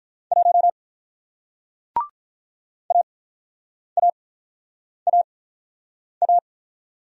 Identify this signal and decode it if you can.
Morse code, heard as a keyed tone, about 48 wpm, spaced as for 9 wpm, 690 Hz. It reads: 1 AAAA